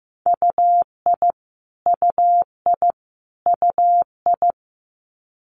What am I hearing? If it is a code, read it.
Morse code, heard as a keyed tone, about 15 wpm, 700 Hz. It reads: UI UI UI
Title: Multi-letter Phonemes Repeated 3x 15wpm